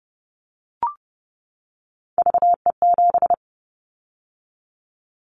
Morse code, heard as a keyed tone, about 30 wpm, 700 Hz. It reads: VE7
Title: Callsign Prefixes for Top 100 DXCC - 30wpm